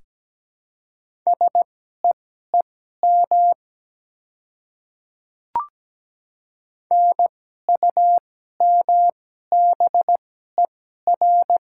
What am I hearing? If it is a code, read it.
Morse code, heard as a keyed tone, about 17 wpm, spaced as for 12 wpm, 700 Hz. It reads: SEEM NUMBER
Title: 2x Character Spacing - Top 300 Words - 17wpm